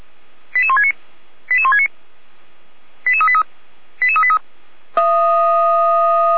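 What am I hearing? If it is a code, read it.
Five-tone selective call (ZVEI), heard as quick runs of five tones, tones of about 70 ms. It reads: 88169 88169 88383 88383
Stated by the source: ZVEI five-tone selective call received on SDR, from feuerwehr-orlamuende.org.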